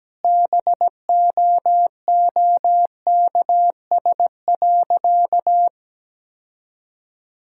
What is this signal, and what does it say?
Morse code, heard as a keyed tone, about 17 wpm, 700 Hz. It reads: BOOKS.
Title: Sentences from Top 1000 Words - Review Part 5 - 17wpm